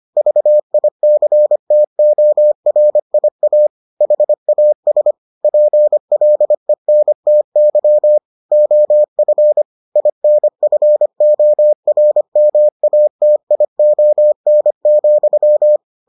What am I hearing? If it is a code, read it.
Morse code, heard as a keyed tone, about 25 wpm, 600 Hz. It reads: VICTORIA HAS PLENTY OF INFORMATION,